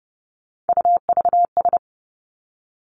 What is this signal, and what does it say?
Morse code, heard as a keyed tone, about 30 wpm, 700 Hz. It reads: UVS
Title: Three Challenging Letter-Number Combinations 30wpm